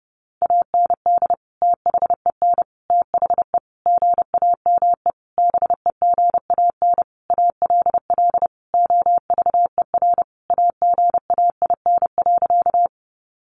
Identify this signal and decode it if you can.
Morse code, heard as a keyed tone, about 30 wpm, 700 Hz. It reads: AND THEN THE GAME BEGAN ALL OVER AGAIN.
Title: Sentences from Top 500 Words 30wpm